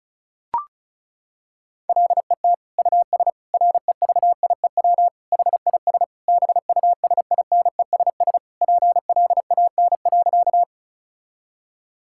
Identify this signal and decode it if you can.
Morse code, heard as a keyed tone, about 36 wpm, 700 Hz. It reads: LET US REVIEW HIS BUSINESS PLAN.
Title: Sentences from Top 1000 Words - Review Part 6 - 35wpm